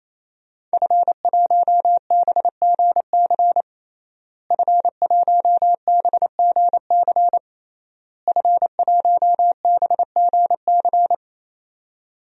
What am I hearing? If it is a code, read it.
Morse code, heard as a keyed tone, about 28 wpm, 700 Hz. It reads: F1BGC F1BGC F1BGC
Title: DXCC-100 Call Signs - 5 Characters - Repeated 3x - 28wpm